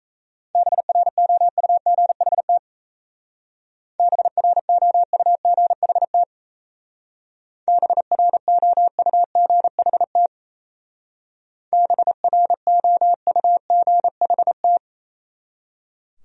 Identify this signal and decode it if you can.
Morse code, heard as a keyed tone, about 36 wpm, 700 Hz. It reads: BROUGHT BROUGHT BROUGHT BROUGHT